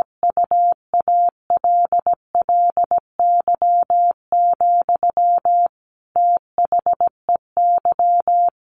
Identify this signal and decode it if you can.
Morse code, heard as a keyed tone, about 17 wpm, 700 Hz. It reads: UALLY, THEY
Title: Sentences from Top 1400 Words 17wpm